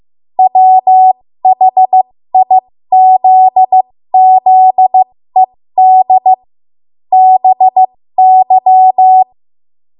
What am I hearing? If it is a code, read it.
Morse code, heard as a keyed tone, about 15 wpm, spaced as for 13 wpm, 750 Hz. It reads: WHIZZED BY
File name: Morse / 13WPM